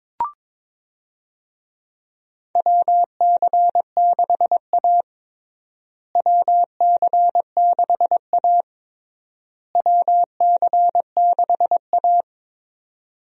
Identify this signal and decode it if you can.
Morse code, heard as a keyed tone, about 22 wpm, 700 Hz. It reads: WC6A WC6A WC6A
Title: USA Call Signs - 4 Characters - Repeated 3x - 22wpm